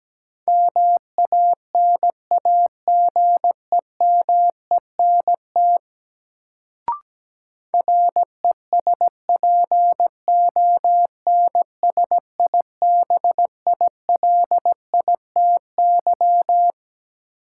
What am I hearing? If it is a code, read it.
Morse code, heard as a keyed tone, about 17 wpm, 700 Hz. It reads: MANAGEMENT RESPONSIBILITY
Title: Sentences from Top 1200 Words - New Word Drill Alt - 17wpm